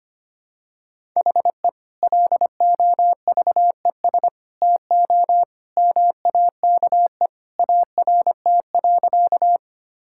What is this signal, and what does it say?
Morse code, heard as a keyed tone, about 25 wpm, 700 Hz. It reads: HE LOVES TO MAKE ART.
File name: practice / 25wpm / Sentences with 2-Word Phrases - 25wpm